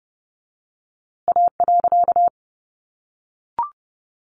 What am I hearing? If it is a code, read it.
Morse code, heard as a keyed tone, about 30 wpm, 700 Hz. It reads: A.
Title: A Sound Of Thunder 30wpm